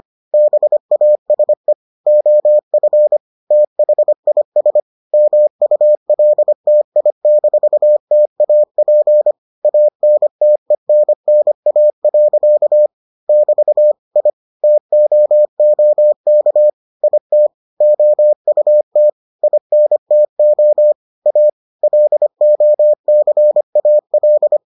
Morse code, heard as a keyed tone, about 25 wpm, 600 Hz. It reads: BASE OF THIS MULTI-TAP ANTENNA. <BT> I TOOK IT OUT INTO A LOCAL